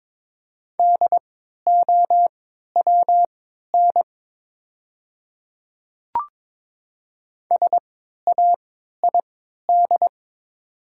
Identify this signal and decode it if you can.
Morse code, heard as a keyed tone, about 22 wpm, spaced as for 12 wpm, 690 Hz. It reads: DOWN SAID
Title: 3x Character Spacing - Top 200 Words - 22wpm